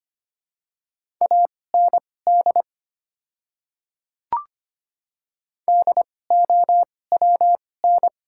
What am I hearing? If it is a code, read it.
Morse code, heard as a keyed tone, about 25 wpm, spaced as for 18 wpm, 700 Hz. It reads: AND DOWN
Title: 2x Character Spacing - Top 100 Words - 25wpm